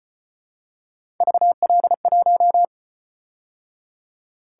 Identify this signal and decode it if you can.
Morse code, heard as a keyed tone, about 28 wpm, 700 Hz. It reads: VL1
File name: Speed-Racing - Three Letter-Number Combinations - 25wpm